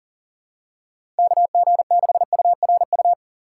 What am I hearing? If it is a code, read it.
Morse code, heard as a keyed tone, about 40 wpm, 700 Hz. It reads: KG6URU
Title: USA Call Signs - 2x Character Spacing - 40wpm